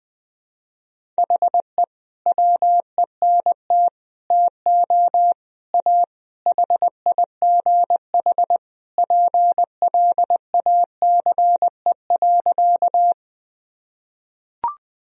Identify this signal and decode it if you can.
Morse code, heard as a keyed tone, about 20 wpm, 700 Hz. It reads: HE WENT TO A HIGH PLACE.
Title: Sentences from Top 1000 Words - Review Part 1 - 20wpm